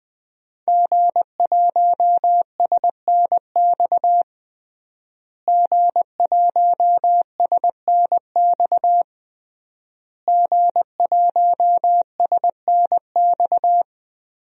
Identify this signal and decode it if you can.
Morse code, heard as a keyed tone, about 20 wpm, 700 Hz. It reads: G1SNX G1SNX G1SNX
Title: DXCC-100 Call Signs - 5 Characters - Repeated 3x - 20wpm